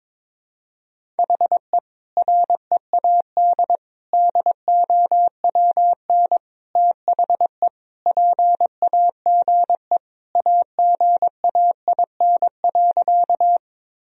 Morse code, heard as a keyed tone, about 22 wpm, 700 Hz. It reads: HE READ DOWN THE PAGE AGAIN.
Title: Sentences from Top 1000 Words - Review Part 1 - 22wpm